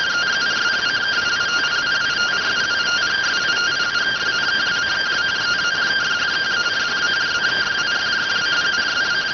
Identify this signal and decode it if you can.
Two-tone teleprinter signal, about 170 Hz shift.